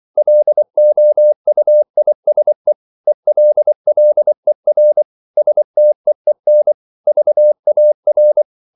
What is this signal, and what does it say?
Morse code, heard as a keyed tone, about 24 wpm, 600 Hz. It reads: LOUISE ELLER STEEN VAR
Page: Ferie 24 wpm